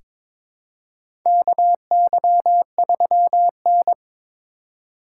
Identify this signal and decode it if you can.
Morse code, heard as a keyed tone, about 22 wpm, 700 Hz. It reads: KY3N
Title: USA Call Signs - 4 Characters - 4x Character Spacing - 22wpm